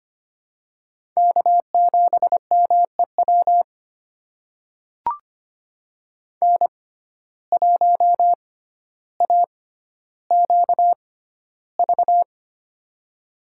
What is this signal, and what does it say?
Morse code, heard as a keyed tone, about 25 wpm, spaced as for 13 wpm, 700 Hz. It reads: K7MEW N1AQV
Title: USA Call Signs - 5 Characters - 6x Character Spacing - 25wpm